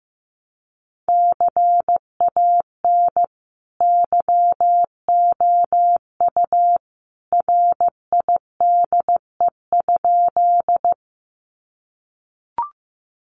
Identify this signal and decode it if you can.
Morse code, heard as a keyed tone, about 15 wpm, 700 Hz. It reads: CAN YOU RIDE?
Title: Sentences from Top 1200 Words 15wpm